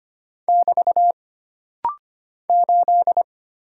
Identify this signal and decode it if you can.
Morse code, heard as a keyed tone, about 25 wpm, 700 Hz. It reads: <BT> 8
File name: Single Character (Letter:Number:Punctuation:Prosign) - Rapid-Fire - 25wpm